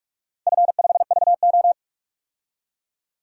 Morse code, heard as a keyed tone, about 42 wpm, 700 Hz. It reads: F5VO
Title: DXCC-100 Call Signs - 4 Characters - 45wpm